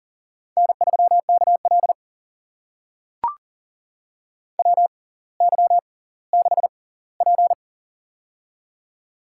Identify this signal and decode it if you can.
Morse code, heard as a keyed tone, about 39 wpm, spaced as for 20 wpm, 700 Hz. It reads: N3KL WY6P
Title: USA Call Signs - 4 Characters - 6x Character Spacing - 40wpm